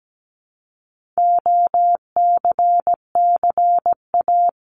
Morse code, heard as a keyed tone, about 17 wpm, 700 Hz. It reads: OCCA
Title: LY Suffix - Top 100 Words with Base Words 17wpm